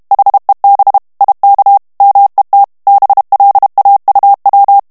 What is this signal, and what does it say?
Morse code, heard as a keyed tone, about 32 wpm, 800 Hz. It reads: HEB IK MET BLAUW
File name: morse mp3 / HOLLANDAIS texte clair 1 vitesse 30 mots minute